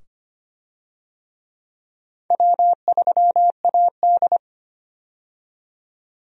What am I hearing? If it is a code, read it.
Morse code, heard as a keyed tone, about 25 wpm, 700 Hz. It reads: W3AD
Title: USA Call Signs - 4 Characters - 25wpm